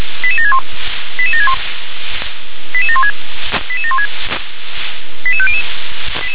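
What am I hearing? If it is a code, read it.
Five-tone selective call (ZVEI), heard as quick runs of five tones, tones of about 80 ms. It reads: 88761 88761 88716 88716 88500
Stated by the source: Five-tone ZVEI selective call 88799, received via SDR.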